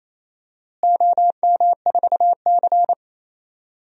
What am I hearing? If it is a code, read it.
Morse code, heard as a keyed tone, about 28 wpm, 700 Hz. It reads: OM4C
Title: DXCC-100 Call Signs - 4 Characters - Repeated 3x - 28wpm